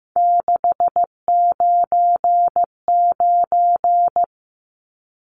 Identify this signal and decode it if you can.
Morse code, heard as a keyed tone, about 15 wpm, 700 Hz. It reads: 699
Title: Numbers - 3-Digit 15wpm